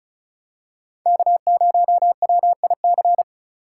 Morse code, heard as a keyed tone, about 36 wpm, 700 Hz. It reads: K0WIC
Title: USA Call Signs - 5 Characters - 4x Character Spacing - 35wpm